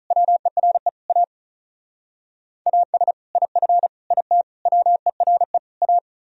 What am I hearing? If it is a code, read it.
Morse code, heard as a keyed tone, about 46 wpm, 700 Hz. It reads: WERE A AS IF IT WERE A